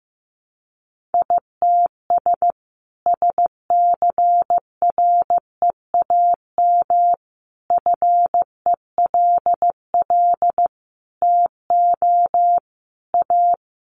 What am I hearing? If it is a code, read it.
Morse code, heard as a keyed tone, about 15 wpm, 700 Hz. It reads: ITS SCREAM FELL TO A
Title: A Sound Of Thunder 15wpm